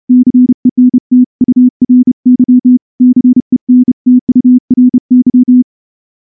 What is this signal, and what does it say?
Morse code, heard as a keyed tone, about 30 wpm, 250 Hz. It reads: CENTURY CENTURY
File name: morse-training-30WPM